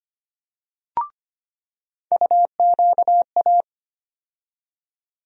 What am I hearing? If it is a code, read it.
Morse code, heard as a keyed tone, about 25 wpm, 700 Hz. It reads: UQA
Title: USA Call Sign Postfixes - 3 Letters 25wpm